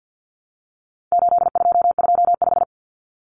Morse code, heard as a keyed tone, about 50 wpm, 700 Hz. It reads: 8225
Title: Common Cut Numbers - 4-Digit 50wpm